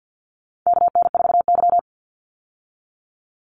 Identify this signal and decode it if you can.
Morse code, heard as a keyed tone, about 50 wpm, 700 Hz. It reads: KN4Y
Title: Call Signs Repeated 3x - 50wpm